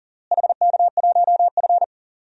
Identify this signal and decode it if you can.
Morse code, heard as a keyed tone, about 40 wpm, 700 Hz. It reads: HK1F
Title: DXCC-100 Call Signs - 4 Characters - Repeated 3x - 40wpm